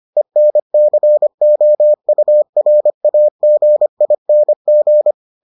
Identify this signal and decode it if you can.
Morse code, heard as a keyed tone, about 25 wpm, 600 Hz. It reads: ENCOURAGING